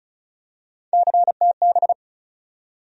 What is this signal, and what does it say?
Morse code, heard as a keyed tone, about 34 wpm, 700 Hz. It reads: CTB
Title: USA Call Sign Postfixes - 3 Letters 35wpm